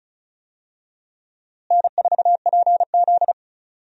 Morse code, heard as a keyed tone, about 36 wpm, 700 Hz. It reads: N4PZ
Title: USA Call Signs - 4 Characters - Repeated 2x - 35wpm